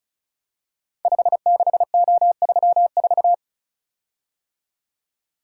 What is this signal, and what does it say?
Morse code, heard as a keyed tone, about 36 wpm, 700 Hz. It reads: 56O34
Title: Code Groups - 5 Letter:Numbers - 35wpm